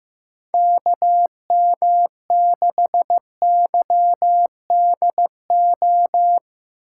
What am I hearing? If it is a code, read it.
Morse code, heard as a keyed tone, about 15 wpm, 700 Hz. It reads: KM6YDO
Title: USA Call Signs - 4x Character Spacing - 15wpm